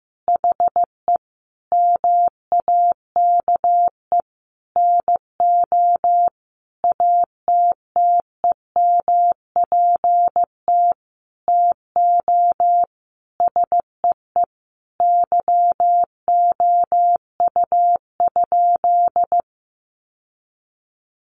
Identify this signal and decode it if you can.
Morse code, heard as a keyed tone, about 15 wpm, 700 Hz. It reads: HE MAKE NO ATTEMPT TO SEE YOU?
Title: Sentences from Top 1000 Words 15wpm